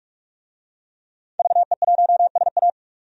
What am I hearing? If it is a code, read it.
Morse code, heard as a keyed tone, about 46 wpm, 700 Hz. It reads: VE1SA
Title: DXCC-100 Call Signs - 5 Characters - 45wpm